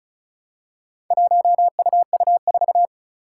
Morse code, heard as a keyed tone, about 36 wpm, 700 Hz. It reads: 1UU4
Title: Four Challenging Letter-Number Combinations 35wpm